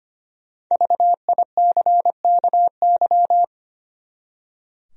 Morse code, heard as a keyed tone, about 25 wpm, 700 Hz. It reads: VICKY
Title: USA Common Names - Max 5 Letters - 25wpm